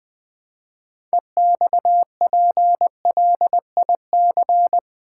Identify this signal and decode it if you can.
Morse code, heard as a keyed tone, about 20 wpm, 700 Hz. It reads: EXPLIC
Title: Build A Word - 3 Letters At A Time - 8-Letter Words - 20wpm